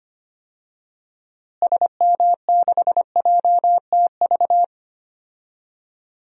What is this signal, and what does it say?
Morse code, heard as a keyed tone, about 25 wpm, 700 Hz. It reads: SM6JTV
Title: DXCC-100 Call Signs - 6 Characters - Repeated 2x - 25wpm